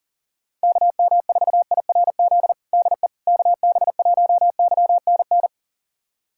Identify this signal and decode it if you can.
Morse code, heard as a keyed tone, about 40 wpm, 700 Hz. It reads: KM4IRZ DE KB1YNN
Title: USA Callsign DE Callsign - 6 Characters - 40wpm